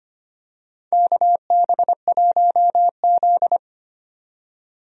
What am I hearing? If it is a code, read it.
Morse code, heard as a keyed tone, about 25 wpm, 700 Hz. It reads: KB1Z
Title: USA Call Signs - 4 Characters - 25wpm